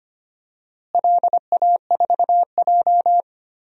Morse code, heard as a keyed tone, about 25 wpm, 700 Hz. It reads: LA4J
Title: DXCC-100 Call Signs - 4 Characters - Repeated 2x - 25wpm